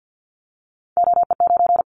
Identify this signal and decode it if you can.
Morse code, heard as a keyed tone, about 50 wpm, 700 Hz. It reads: OE9
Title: Callsign Prefixes for Top 50 DXCC - 50wpm